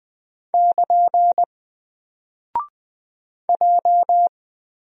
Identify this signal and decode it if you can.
Morse code, heard as a keyed tone, about 20 wpm, 700 Hz. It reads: <KN> J
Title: Single Character (Letter:Number:Punctuation:Prosign) 20wpm